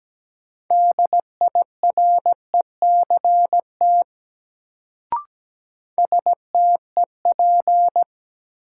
Morse code, heard as a keyed tone, about 17 wpm, 700 Hz. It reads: DIRECT STEP
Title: Sentences from Top 400 Words - New Word Drill - 17wpm